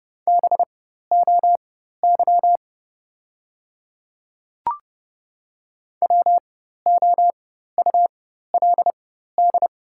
Morse code, heard as a keyed tone, about 29 wpm, spaced as for 14 wpm, 700 Hz. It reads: BOY WOULD
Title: 4x Character Spacing - Top 300 Words - 30wpm